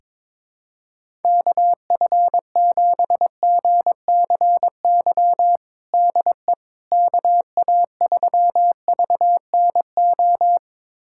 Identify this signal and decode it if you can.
Morse code, heard as a keyed tone, about 22 wpm, 700 Hz. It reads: KF7GCY DE KA3VNO
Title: USA Callsign DE Callsign - 6 Characters - 22wpm